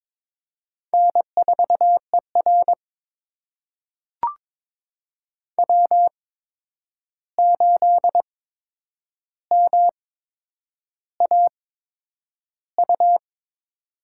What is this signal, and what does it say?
Morse code, heard as a keyed tone, about 22 wpm, spaced as for 9 wpm, 700 Hz. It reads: N4ER W8MAU
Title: USA Call Signs - 8x Character Spacing - 22wpm